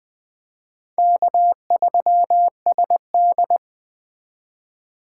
Morse code, heard as a keyed tone, about 20 wpm, 700 Hz. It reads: K3SD
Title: Speed Racing - USA 1x2 Call Signs 17wpm